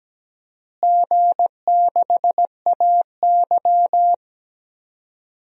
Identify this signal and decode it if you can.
Morse code, heard as a keyed tone, about 17 wpm, 700 Hz. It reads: G6AY
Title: DXCC-100 Call Signs - 4 Characters - Repeated 3x - 17wpm